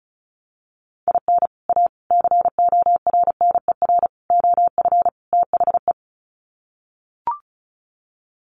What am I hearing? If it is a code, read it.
Morse code, heard as a keyed tone, about 34 wpm, 700 Hz. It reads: IN A CORNER OF THE
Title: Sets of 5 Words Encore2 35wpm